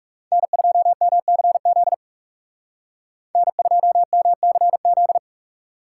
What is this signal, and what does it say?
Morse code, heard as a keyed tone, about 35 wpm, 700 Hz. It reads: N2MCZ N2MCZ
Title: Speed Racing - USA 1x3 Call Signs 30wpm